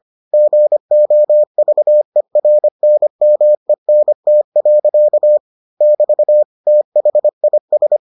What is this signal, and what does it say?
Morse code, heard as a keyed tone, about 25 wpm, 600 Hz. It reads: GOVERNMENT. <BT> THIS